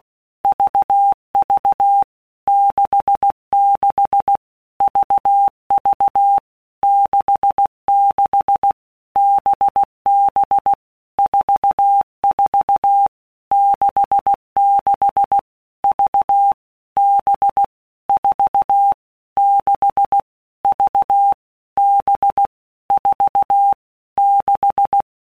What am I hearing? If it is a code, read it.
Morse code, heard as a keyed tone, about 16 wpm, 800 Hz. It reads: VV 66 VV 66 BB 44 66 V B 4 6 V B 4 6
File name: Basic Letter Groups By WPM's / Groups@15 / VB46@15